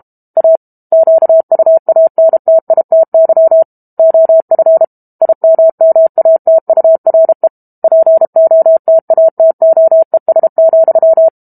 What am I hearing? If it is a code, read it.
Morse code, heard as a keyed tone, about 32 wpm, 650 Hz. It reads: A QUANTITY OF IMMATURE POTATOES,
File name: Book Two / 06-Chapter 6 - 32 WPM